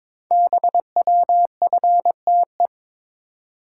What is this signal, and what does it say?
Morse code, heard as a keyed tone, about 22 wpm, 700 Hz. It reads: BWFTE
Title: Code Groups - 5 Letters - 22wpm